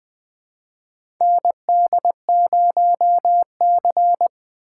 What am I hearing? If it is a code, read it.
Morse code, heard as a keyed tone, about 20 wpm, 700 Hz. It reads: ND0C
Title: USA Call Signs - 4 Characters - 8x Character Spacing - 20wpm